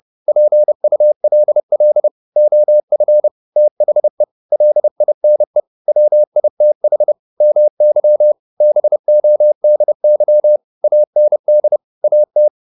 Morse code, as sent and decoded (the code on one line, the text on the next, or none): .--. ..- .-.. .-.. / --- ..-. / - .... / .-.. .. -. / .-- .. - .... / -- -.-- / -... --- -.. -.-- / .- -. -.. / .- -
PULL OF THE LINE WITH MY BODY AND AT